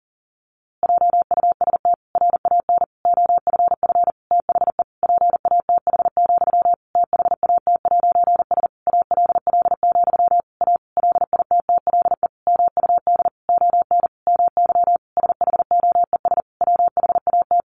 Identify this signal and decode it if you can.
Morse code, heard as a keyed tone, about 40 wpm, 700 Hz. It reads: JUST RAN OFF THE PATH, THAT'S ALL, A LITTLE MUD ON MY SHOES WHAT
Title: A Sound Of Thunder 40wpm